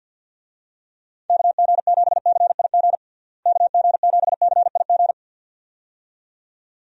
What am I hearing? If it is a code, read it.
Morse code, heard as a keyed tone, about 51 wpm, 700 Hz. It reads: KG7CIG KG7CIG
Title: USA Call Signs - 6 Characters - Repeated 2x - 50wpm